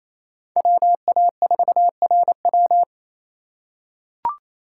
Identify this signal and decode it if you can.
Morse code, heard as a keyed tone, about 28 wpm, 700 Hz. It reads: WA4RW
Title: USA Call Signs - 5 Characters - 4x Character Spacing - 28wpm